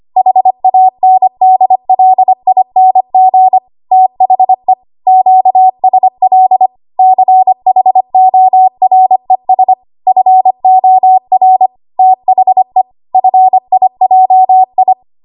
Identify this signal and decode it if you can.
Morse code, heard as a keyed tone, about 25 wpm, 750 Hz. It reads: HANDLING THE QSL CHORES FOR THE FIJI
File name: Archive / 25 WPM / 25WPM